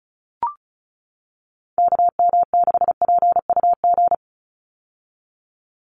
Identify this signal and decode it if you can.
Morse code, heard as a keyed tone, about 36 wpm, 700 Hz. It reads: KM6PUG
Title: USA Call Signs 35wpm